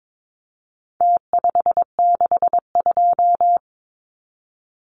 Morse code, heard as a keyed tone, about 22 wpm, 700 Hz. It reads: T562
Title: Numbers - Contest Style - 4-Digit 22wpm